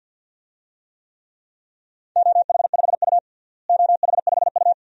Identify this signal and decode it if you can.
Morse code, heard as a keyed tone, about 51 wpm, 700 Hz. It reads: OH5U OH5U
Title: DXCC-100 Call Signs - 4 Characters - Repeated 2x - 50wpm